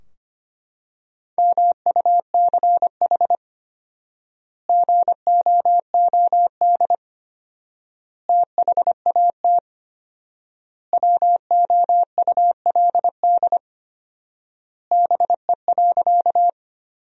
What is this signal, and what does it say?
Morse code, heard as a keyed tone, about 25 wpm, 700 Hz. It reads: MUCH GOOD THAT WOULD BE.
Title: 4x Word-Spacing - Sentences from Top 200 Words - 25wpm